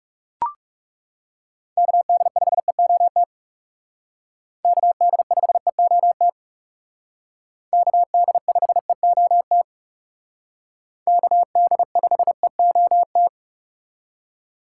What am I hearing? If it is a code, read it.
Morse code, heard as a keyed tone, about 37 wpm, 700 Hz. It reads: KD5EOT KD5EOT KD5EOT KD5EOT
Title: Speed Racing - USA 2x3 Call Signs 30wpm